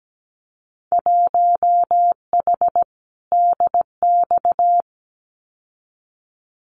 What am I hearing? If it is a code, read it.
Morse code, heard as a keyed tone, about 17 wpm, 700 Hz. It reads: 1H DX
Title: Winter Field Day Exchanges 17wpm